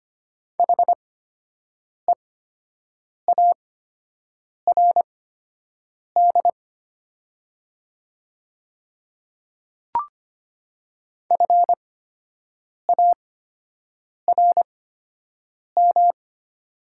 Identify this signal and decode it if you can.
Morse code, heard as a keyed tone, about 25 wpm, spaced as for 7 wpm, 700 Hz. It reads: HEARD FARM
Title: 8x Character Spacing - Top 500 Words - 25wpm